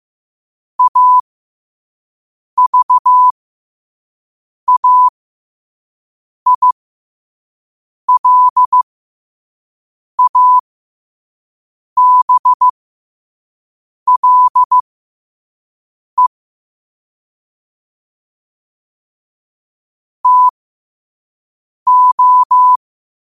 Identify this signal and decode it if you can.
Morse code, heard as a keyed tone, about 15 wpm, spaced as for 5 wpm, 1000 Hz. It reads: AVAILABLE TO
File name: qotdfiles / 5wpm